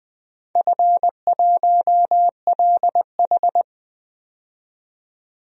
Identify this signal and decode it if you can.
Morse code, heard as a keyed tone, about 20 wpm, 700 Hz. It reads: F1LH
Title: DXCC-100 Call Signs - 4 Characters - 20wpm